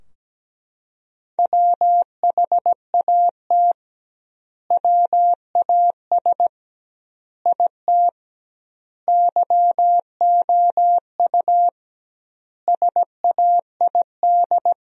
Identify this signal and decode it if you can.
Morse code, heard as a keyed tone, about 17 wpm, 700 Hz. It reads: WHAT WAS IT YOU SAID
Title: 2x Word-Spacing - Sentences from Top 200 Words - 17wpm